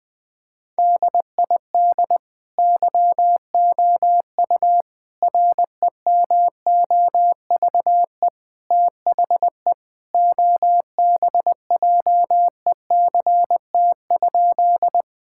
Morse code, heard as a keyed tone, about 20 wpm, 700 Hz. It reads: DID YOU REMOVE THE OBJECT?
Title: Sentences from Top 1000 Words - Review Part 5 - 20wpm